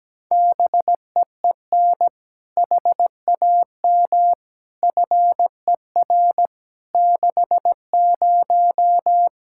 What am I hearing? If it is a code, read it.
Morse code, heard as a keyed tone, about 17 wpm, 700 Hz. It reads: BEEN HAM FER 60
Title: Practice QSO - Exchange 3 - More Info - 17wpm